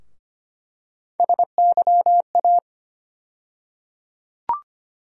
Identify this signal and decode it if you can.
Morse code, heard as a keyed tone, about 25 wpm, 700 Hz. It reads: SYA
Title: USA Call Sign Postfixes - 3 Letters 25wpm